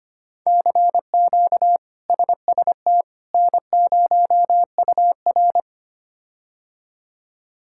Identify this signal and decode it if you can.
Morse code, heard as a keyed tone, about 25 wpm, 700 Hz. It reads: CQ SST N0UR
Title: K1USN SST QSO Practice - 25wpm